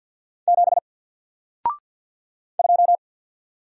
Morse code, heard as a keyed tone, about 50 wpm, 700 Hz. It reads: 7 2
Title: Single Letter-Number Alt 50wpm